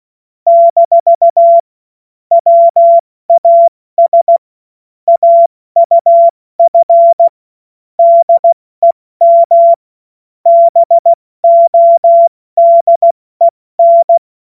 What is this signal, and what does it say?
Morse code, heard as a keyed tone, about 16 wpm, 680 Hz. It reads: - WAS AUF DEM BODEN
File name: Morsekurs / Witze / Witz 17. August 2021, 70 F80